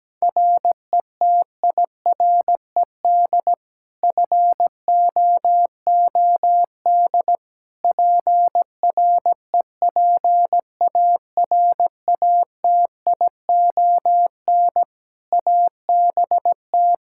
Morse code, heard as a keyed tone, about 17 wpm, 700 Hz. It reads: RETIRED FOOD PREPARATION ABT
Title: K1USN SST QSO Practice 17wpm